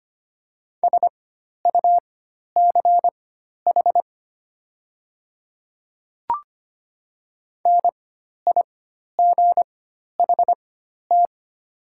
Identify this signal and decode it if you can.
Morse code, heard as a keyed tone, about 25 wpm, spaced as for 12 wpm, 700 Hz. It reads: SUCH NIGHT
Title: 4x Character Spacing - Top 300 Words - 25wpm